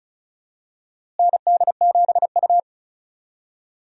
Morse code, heard as a keyed tone, about 36 wpm, 700 Hz. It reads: ND7U
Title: USA Call Signs - 4 Characters - Repeated 3x - 35wpm